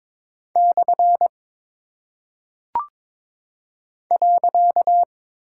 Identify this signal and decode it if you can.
Morse code, heard as a keyed tone, about 22 wpm, 700 Hz. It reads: / .
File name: practice / 22wpm / Single Punctuation-Prosign - 22wpm